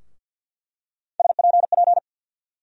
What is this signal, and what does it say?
Morse code, heard as a keyed tone, about 47 wpm, 700 Hz. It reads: SPP
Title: USA Call Sign Postfixes - 3 Letters 50wpm